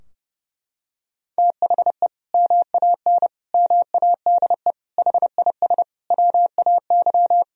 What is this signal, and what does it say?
Morse code, heard as a keyed tone, about 30 wpm, 700 Hz. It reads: THE MAN MADE HIS WAY
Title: Sentences from Top 1000 Words - Review Part 2 - 30wpm